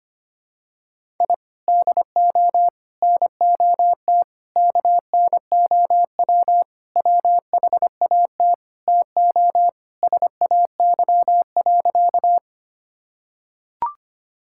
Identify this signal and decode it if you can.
Morse code, heard as a keyed tone, about 25 wpm, 700 Hz. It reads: I DO NOT KNOW WHAT TO SAY.
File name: practice / 25wpm / Sentences with 4-Word Phrases - 25wpm